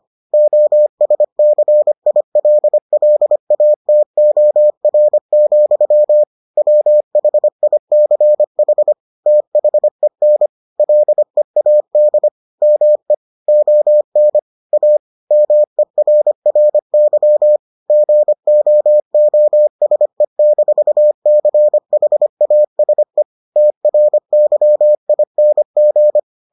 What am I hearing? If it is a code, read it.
Morse code, heard as a keyed tone, about 25 wpm, 600 Hz. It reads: OSCILLATOR, WHICH THEN LEAD ME ON A MERRY GOOSE-CHASE TRYING